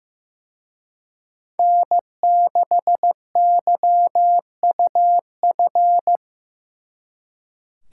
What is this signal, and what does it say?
Morse code, heard as a keyed tone, about 18 wpm, 700 Hz. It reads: N6YUF
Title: Speed Racing - USA 1x3 Call Signs 15wpm